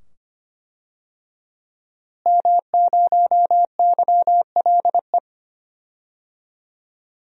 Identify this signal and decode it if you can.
Morse code, heard as a keyed tone, about 25 wpm, 700 Hz. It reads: M0YLE
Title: DXCC-100 Call Signs - 5 Characters - Repeated 2x - 25wpm